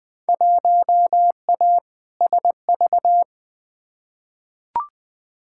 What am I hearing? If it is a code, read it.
Morse code, heard as a keyed tone, about 20 wpm, 700 Hz. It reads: 1A SV
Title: Field Day Exchanges - 20wpm